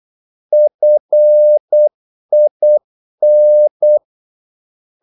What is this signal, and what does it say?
Morse code, heard as a keyed tone, about 8 wpm, 600 Hz. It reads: FIN
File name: Korte ord DK 8wpm